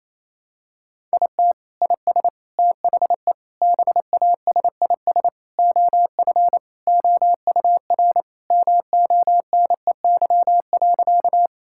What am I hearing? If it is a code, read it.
Morse code, heard as a keyed tone, about 27 wpm, 700 Hz. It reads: IT IS THE BASIS OF OUR MONEY.
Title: Sentences from Top 1000 Words - Review Part 5 - 28wpm